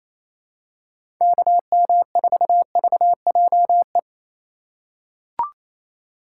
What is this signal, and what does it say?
Morse code, heard as a keyed tone, about 28 wpm, 700 Hz. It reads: KM4VJE
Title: USA Call Signs - 6 Characters - 4x Character Spacing - 28wpm